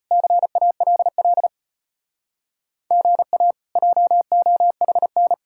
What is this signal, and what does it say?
Morse code, heard as a keyed tone, about 33 wpm, 700 Hz. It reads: CALL GA JOHN